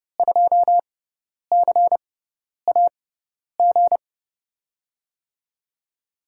Morse code, heard as a keyed tone, about 32 wpm, spaced as for 10 wpm, 700 Hz. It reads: 2CAG
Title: USA Call Signs - 6x Character Spacing - 30wpm